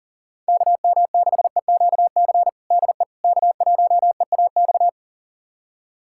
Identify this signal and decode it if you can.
Morse code, heard as a keyed tone, about 37 wpm, 700 Hz. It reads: KM6EQC DE K1EAX
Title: USA Callsign DE Callsign 40wpm